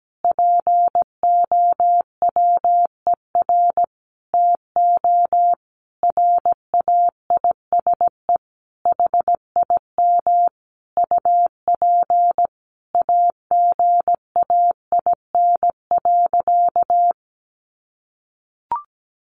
Morse code, heard as a keyed tone, about 17 wpm, 700 Hz. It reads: POWER TO RAISE HIM UP AGAIN.
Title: Sentences from Top 1400 Words 17wpm